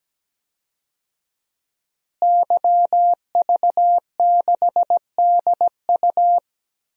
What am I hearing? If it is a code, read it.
Morse code, heard as a keyed tone, about 17 wpm, 700 Hz. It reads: YV6DU
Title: DXCC-100 Call Signs - 5 Characters - Repeated 3x - 17wpm